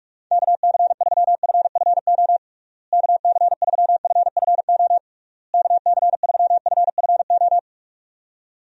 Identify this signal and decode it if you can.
Morse code, heard as a keyed tone, about 46 wpm, 700 Hz. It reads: KC3FFO KC3FFO KC3FFO
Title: USA Call Signs - 6 Characters - Repeated 3x - 45wpm